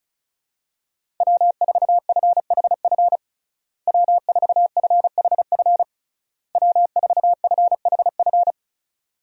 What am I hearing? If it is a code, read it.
Morse code, heard as a keyed tone, about 34 wpm, 700 Hz. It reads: W4FHF W4FHF W4FHF
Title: USA Call Signs - 5 Characters - Repeated 3x - 35wpm